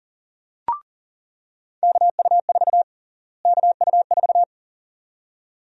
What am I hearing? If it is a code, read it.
Morse code, heard as a keyed tone, about 40 wpm, 700 Hz. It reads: KU4 KU4
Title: USA Call Sign Prefixes - Repeated 2x - 40wpm